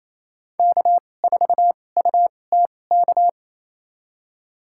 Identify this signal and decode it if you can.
Morse code, heard as a keyed tone, about 28 wpm, spaced as for 20 wpm, 700 Hz. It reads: K4UTK
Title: USA Call Signs - 5 Characters - 2x Character Spacing - 28wpm